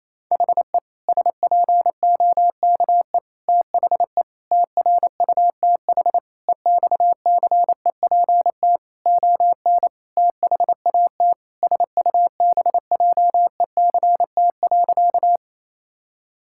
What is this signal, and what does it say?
Morse code, heard as a keyed tone, about 28 wpm, 700 Hz. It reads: HE SPOKE THE TRUTH EXCEPT ON THAT SUBJECT.
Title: Sentences from Top 1600 Words 28wpm